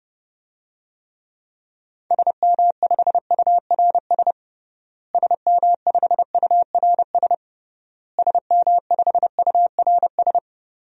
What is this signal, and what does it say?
Morse code, heard as a keyed tone, about 32 wpm, 700 Hz. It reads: SM5URS SM5URS SM5URS
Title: DXCC-100 Call Signs - 6 Characters - Repeated 3x - 30wpm